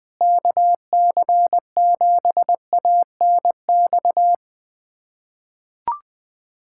Morse code, heard as a keyed tone, about 20 wpm, 700 Hz. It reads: KC7ANX
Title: USA Callsign DE Callsign 20wpm